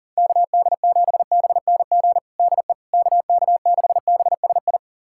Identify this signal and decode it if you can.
Morse code, heard as a keyed tone, about 37 wpm, 700 Hz. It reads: KD7BNG DE KK6BSI
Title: USA Callsign DE Callsign - 6 Characters - 40wpm